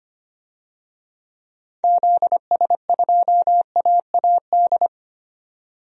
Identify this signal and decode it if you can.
Morse code, heard as a keyed tone, about 25 wpm, 700 Hz. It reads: ZS2AAD
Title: DXCC-100 Call Signs - 6 Characters - 25wpm